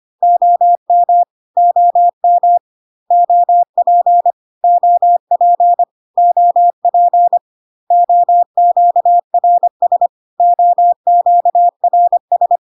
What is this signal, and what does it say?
Morse code, heard as a keyed tone, about 25 wpm, 700 Hz. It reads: OM OM OP OP OP OQRS OQRS
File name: cw / CW common abbreviations alphabetical repeated 3 times 25 25wpm ik1hge